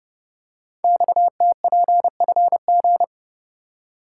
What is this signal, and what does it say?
Morse code, heard as a keyed tone, about 30 wpm, 700 Hz. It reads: XTPFG
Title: Code Groups - 5 Letters - 30wpm